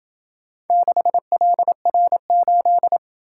Morse code, heard as a keyed tone, about 27 wpm, 700 Hz. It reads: 6LR8